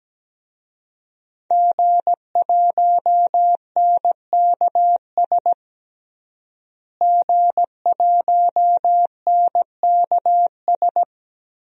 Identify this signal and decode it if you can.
Morse code, heard as a keyed tone, about 17 wpm, 700 Hz. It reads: G1NKS G1NKS
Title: DXCC-100 Call Signs - 5 Characters - Repeated 3x - 17wpm